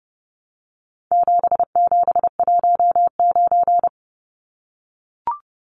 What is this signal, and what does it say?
Morse code, heard as a keyed tone, about 30 wpm, 700 Hz. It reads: 7719
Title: Numbers - 4-Digit 30wpm